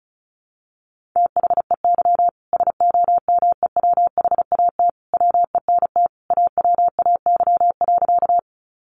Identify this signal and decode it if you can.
Morse code, heard as a keyed tone, about 34 wpm, 700 Hz. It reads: THEY SOMEWHAT WENT AWAY.
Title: Sentences from Top 1300 Words 35wpm